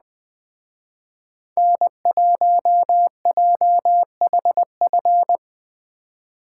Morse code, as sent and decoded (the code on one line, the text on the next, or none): -. .---- .--- .... ..-.
N1JHF